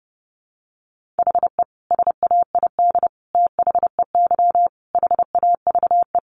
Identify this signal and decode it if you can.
Morse code, heard as a keyed tone, about 30 wpm, 700 Hz. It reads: HE SAID THEY HAVE
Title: Sentences from Top 800 Words 30wpm